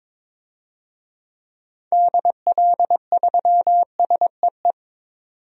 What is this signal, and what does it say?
Morse code, heard as a keyed tone, about 21 wpm, 700 Hz. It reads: DL3SEE
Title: DXCC-100 Call Signs - 6 Characters - 22wpm